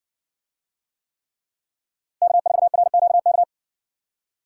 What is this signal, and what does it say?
Morse code, heard as a keyed tone, about 56 wpm, 700 Hz. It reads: K4RQK